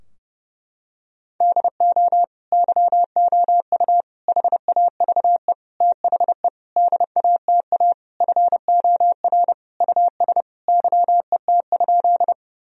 Morse code, heard as a keyed tone, about 30 wpm, 700 Hz. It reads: DO YOU HAVE THE DATA FOR US YET?
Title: Sentences from Top 1000 Words - Review Part 3 - 30wpm